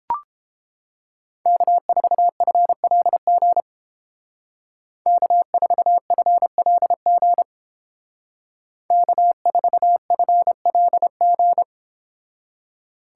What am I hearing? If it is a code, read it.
Morse code, heard as a keyed tone, about 29 wpm, 700 Hz. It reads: K4FLG K4FLG K4FLG